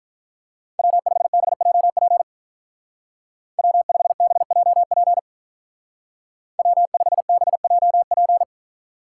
Morse code, heard as a keyed tone, about 43 wpm, 700 Hz. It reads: W5BJP W5BJP W5BJP
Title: Speed Racing - USA 1x3 Call Signs 35wpm